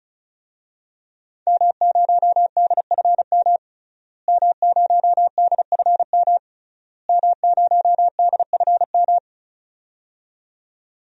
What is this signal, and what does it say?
Morse code, heard as a keyed tone, about 36 wpm, 700 Hz. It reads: M0DFM M0DFM M0DFM
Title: DXCC-100 Call Signs - 5 Characters - Repeated 3x - 35wpm